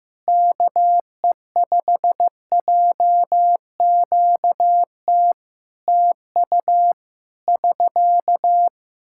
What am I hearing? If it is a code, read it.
Morse code, heard as a keyed tone, about 15 wpm, 700 Hz. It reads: KE5JQT TU <SK>
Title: Practice QSO - Exchange 4 - Ending - 15wpm